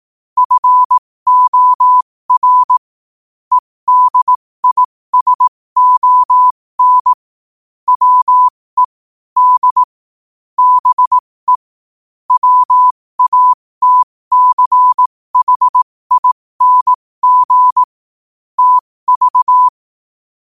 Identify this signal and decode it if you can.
Morse code, heard as a keyed tone, about 18 wpm, spaced as for 15 wpm, 1000 Hz. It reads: FOR EDISON WE D BE WATCHING TV